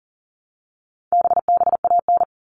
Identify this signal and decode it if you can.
Morse code, heard as a keyed tone, about 40 wpm, 700 Hz. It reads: BBAN
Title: All Cut Numbers - 4-Digit 40wpm